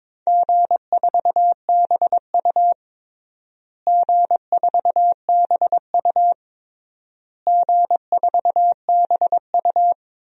Morse code, heard as a keyed tone, about 22 wpm, 700 Hz. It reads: G4BU G4BU G4BU
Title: DXCC-100 Call Signs - 4 Characters - Repeated 3x - 22wpm